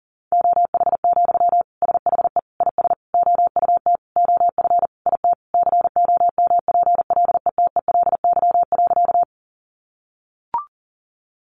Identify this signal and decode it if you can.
Morse code, heard as a keyed tone, about 40 wpm, 700 Hz. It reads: OH, SHE IS OUT OF IT COMPLETELY.
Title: Sentences from Top 1100 Words 40wpm